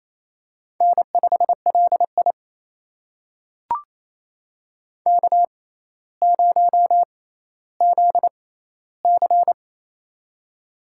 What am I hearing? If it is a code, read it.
Morse code, heard as a keyed tone, about 28 wpm, spaced as for 14 wpm, 700 Hz. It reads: N5LI K0ZC
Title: USA Call Signs - 6x Character Spacing - 28wpm